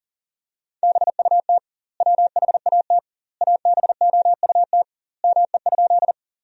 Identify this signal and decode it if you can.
Morse code, heard as a keyed tone, about 40 wpm, 700 Hz. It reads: BUT WHAT ABOUT ME?
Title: 2x Word-Spacing - Sentences from Top 200 Words - 40wpm